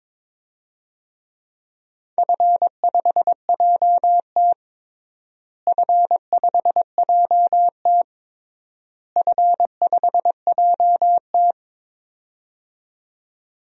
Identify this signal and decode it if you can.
Morse code, heard as a keyed tone, about 22 wpm, 700 Hz. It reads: F5JT F5JT F5JT
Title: DXCC-100 Call Signs - 4 Characters - Repeated 3x - 22wpm